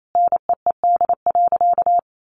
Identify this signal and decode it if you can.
Morse code, heard as a keyed tone, about 28 wpm, 700 Hz. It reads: NEED.
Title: Sentences from Top 1200 Words 28wpm